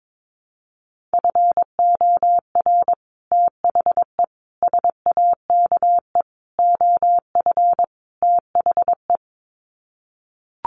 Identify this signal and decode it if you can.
Morse code, heard as a keyed tone, about 22 wpm, 700 Hz. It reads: FOR THE SAKE OF THE
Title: Sets of 5 Words Encore 22wpm